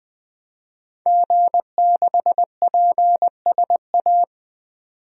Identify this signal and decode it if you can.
Morse code, heard as a keyed tone, about 20 wpm, 700 Hz. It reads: G6PSA
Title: DXCC-100 Call Signs - 5 Characters - 20wpm